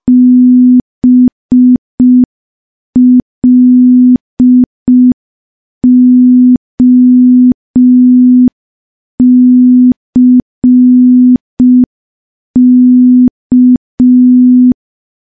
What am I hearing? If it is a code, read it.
Morse code, heard as a keyed tone, about 5 wpm, 250 Hz. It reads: BLOCK
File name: morse / morse-training-05WPM